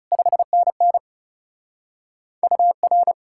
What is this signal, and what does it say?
Morse code, heard as a keyed tone, about 37 wpm, 700 Hz. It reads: 5NN UR
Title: Speed-Racing Common QSO Phrases Encore - 30wpm